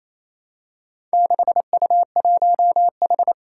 Morse code, heard as a keyed tone, about 27 wpm, 700 Hz. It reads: 6U1H
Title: Four Challenging Letter-Number Combinations 28wpm